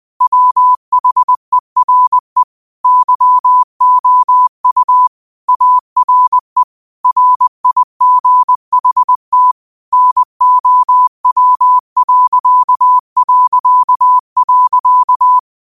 Morse code, heard as a keyed tone, about 20 wpm, 1000 Hz. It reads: WHERE YOU ARE RIGHT NOW...
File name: qotdfiles / 20wpm